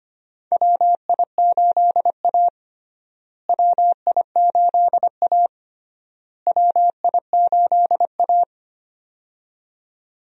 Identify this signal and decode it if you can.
Morse code, heard as a keyed tone, about 25 wpm, 700 Hz. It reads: WI8A WI8A WI8A
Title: USA Call Signs - 4 Characters - Repeated 3x - 25wpm